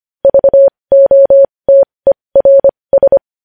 Morse code, heard as a keyed tone, about 25 wpm, spaced as for 20 wpm, 550 Hz. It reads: VOTERS
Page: news-headlines-in-morse-code-at-20-wpm